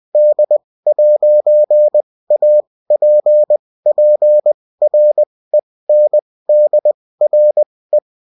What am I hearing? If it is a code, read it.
Morse code, heard as a keyed tone, about 20 wpm, spaced as for 16 wpm, 600 Hz. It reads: D'APPRENDRE